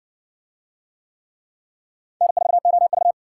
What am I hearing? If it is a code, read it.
Morse code, heard as a keyed tone, about 53 wpm, 700 Hz. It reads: N4OV